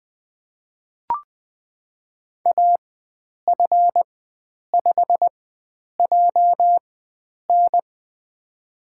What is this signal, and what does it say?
Morse code, heard as a keyed tone, about 20 wpm, spaced as for 9 wpm, 700 Hz. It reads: AF5JN
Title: USA Call Signs - 4x Character Spacing - 20wpm